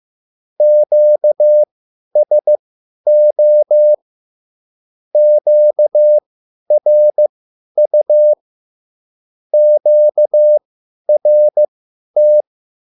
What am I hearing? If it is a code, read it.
Morse code, heard as a keyed tone, about 15 wpm, spaced as for 10 wpm, 600 Hz. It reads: QSO QRU QRT